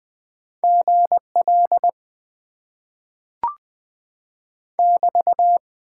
Morse code, heard as a keyed tone, about 20 wpm, 700 Hz. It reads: GL <BT>
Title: Top 90 QSO Elements - 6x Character Spacing - 20wpm